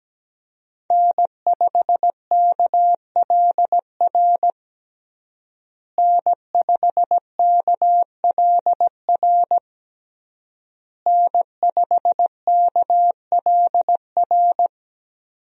Morse code, heard as a keyed tone, about 17 wpm, 700 Hz. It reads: N5KLR N5KLR N5KLR
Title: USA Call Signs - 5 Characters - Repeated 3x - 17wpm